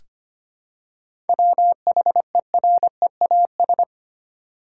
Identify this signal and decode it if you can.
Morse code, heard as a keyed tone, about 25 wpm, 700 Hz. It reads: WHEREAS
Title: Seven Letter Words - 25wpm